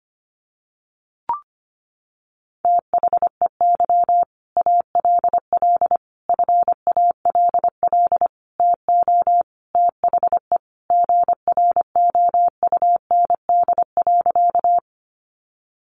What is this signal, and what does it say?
Morse code, heard as a keyed tone, about 25 wpm, 700 Hz. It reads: THEY ALL FALL TO THE GROUND.
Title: Sentences from Top 500 Words 25wpm